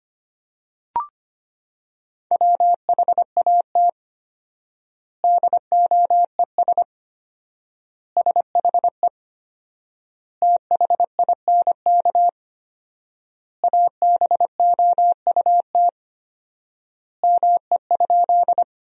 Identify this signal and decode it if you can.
Morse code, heard as a keyed tone, about 25 wpm, 700 Hz. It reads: WHAT DOES SHE THINK ABOUT ME?
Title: 4x Word-Spacing - Sentences from Top 200 Words - 25wpm